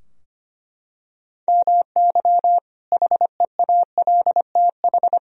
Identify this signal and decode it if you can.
Morse code, heard as a keyed tone, about 25 wpm, 700 Hz. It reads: MY HEALTH
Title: Sentences from Top 1000 Words - Review Part 3 - 25wpm